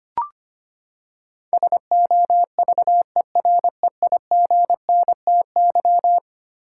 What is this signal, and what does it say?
Morse code, heard as a keyed tone, about 25 wpm, 700 Hz. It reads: SOVEREIGNTY
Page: morse-code-ninja-practice-25wpm